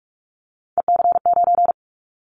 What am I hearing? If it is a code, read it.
Morse code, heard as a keyed tone, about 44 wpm, 700 Hz. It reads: EC9
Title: Callsign Prefixes for Top 100 DXCC - 45wpm